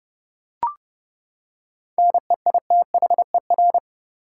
Morse code, heard as a keyed tone, about 32 wpm, 700 Hz. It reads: NEITHER
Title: Seven Letter Words - 30wpm